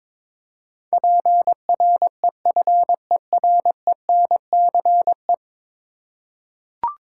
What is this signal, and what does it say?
Morse code, heard as a keyed tone, about 22 wpm, 700 Hz. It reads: PREFERENCE
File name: practice / 22wpm / 10 Letter Words - First 3 Letters Given - 22wpm